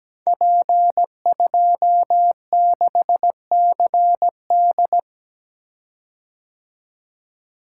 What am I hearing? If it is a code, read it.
Morse code, heard as a keyed tone, about 17 wpm, 700 Hz. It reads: P26CD
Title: Code Groups - 5 Letter:Numbers - 17wpm